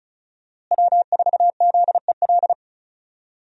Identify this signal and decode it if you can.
Morse code, heard as a keyed tone, about 36 wpm, 700 Hz. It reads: W4ZEL
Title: USA Call Signs - 5 Characters - Repeated 2x - 35wpm